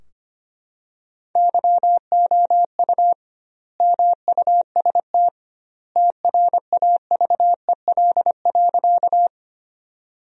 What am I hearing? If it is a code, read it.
Morse code, heard as a keyed tone, about 25 wpm, 700 Hz. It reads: YOU MUST TRAVEL.
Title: 2x Word-Spacing - Sentences from Top 400 Words - 25wpm